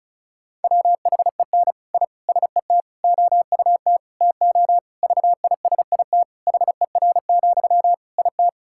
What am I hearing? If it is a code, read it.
Morse code, heard as a keyed tone, about 34 wpm, 700 Hz. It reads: WHEN I SET OUT TO VISIT HER, IT
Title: Sentences from Top 1000 Words - Review Part 6 - 35wpm